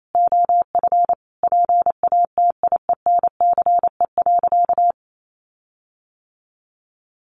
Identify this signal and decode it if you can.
Morse code, heard as a keyed tone, about 28 wpm, 700 Hz. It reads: OF PATIENCE.
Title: Sentences from Top 1300 Words 28wpm